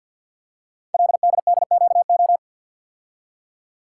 Morse code, heard as a keyed tone, about 51 wpm, 700 Hz. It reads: LDDQO
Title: Code Groups - 5 Letters - 50wpm